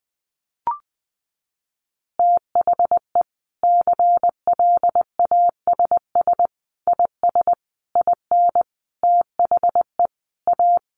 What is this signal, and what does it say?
Morse code, heard as a keyed tone, about 20 wpm, 700 Hz. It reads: THE CLASS IS IN THE A
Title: Additional Easy Phrases 20wpm